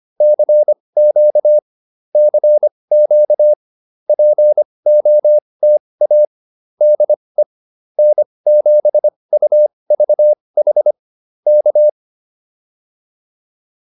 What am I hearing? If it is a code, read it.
Morse code, heard as a keyed tone, about 25 wpm, spaced as for 20 wpm, 600 Hz. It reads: CQ CQ POTA DE N7UVH K